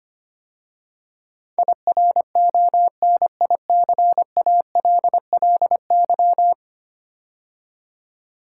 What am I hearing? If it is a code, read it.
Morse code, heard as a keyed tone, about 25 wpm, 700 Hz. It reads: IRONICALLY
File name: 10 Letter Words - First 3 Letters Given - 25wpm